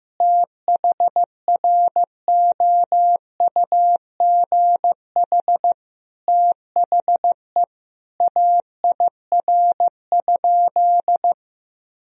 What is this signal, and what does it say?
Morse code, heard as a keyed tone, about 15 wpm, 700 Hz. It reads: THROUGH THE AIR?
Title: Sentences from Top 1000 Words - Review Part 2 - 15wpm